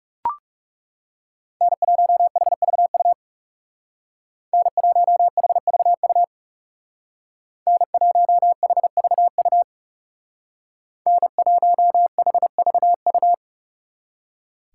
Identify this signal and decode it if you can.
Morse code, heard as a keyed tone, about 40 wpm, 700 Hz. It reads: N1HVU N1HVU N1HVU N1HVU